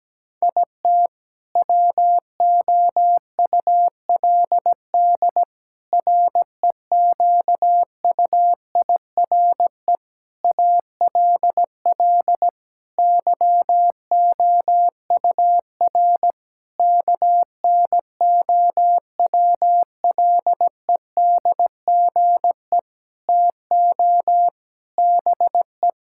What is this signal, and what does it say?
Morse code, heard as a keyed tone, about 17 wpm, 700 Hz. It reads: IT WOULD REQUIRE ALL YOUR KNOWLEDGE TO BE
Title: Sentences from Top 1000 Words - Review Part 6 - 17wpm